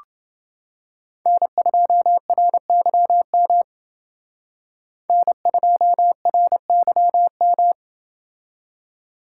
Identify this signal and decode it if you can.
Morse code, heard as a keyed tone, about 28 wpm, 700 Hz. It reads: N2RYM N2RYM